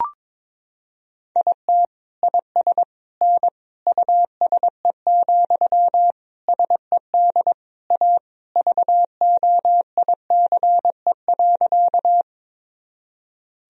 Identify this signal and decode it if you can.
Morse code, heard as a keyed tone, about 22 wpm, 700 Hz. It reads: IT IS N USE, SED A VOICE.
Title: Sentences from Top 500 Words and Abbreviations 22wpm